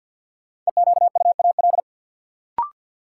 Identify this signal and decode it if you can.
Morse code, heard as a keyed tone, about 49 wpm, 700 Hz. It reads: EQUAL
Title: Sentences from Top 1400 Words - New Word Drill Alt - 50wpm